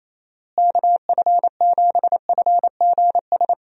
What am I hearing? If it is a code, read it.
Morse code, heard as a keyed tone, about 27 wpm, 700 Hz. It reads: KF7FGS
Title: USA Call Signs - 6 Characters - Repeated 3x - 28wpm